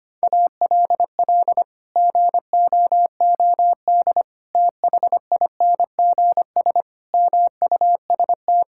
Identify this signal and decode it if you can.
Morse code, heard as a keyed tone, about 25 wpm, 700 Hz. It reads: ALL GOOD THINGS MUST
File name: Sets of 9 Words - 25wpm